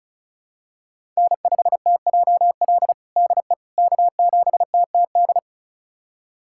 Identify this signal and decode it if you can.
Morse code, heard as a keyed tone, about 34 wpm, 700 Hz. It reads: N5TJL DE K7TTD
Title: USA Callsign DE Callsign - 5 Characters - 35wpm